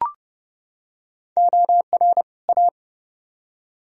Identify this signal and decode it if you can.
Morse code, heard as a keyed tone, about 29 wpm, 700 Hz. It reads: OR A